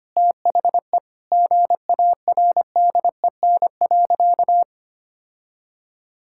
Morse code, heard as a keyed tone, about 25 wpm, 700 Hz. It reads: THE GARDEN.
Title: Sentences from Top 1000 Words - Review Part 6 - 25wpm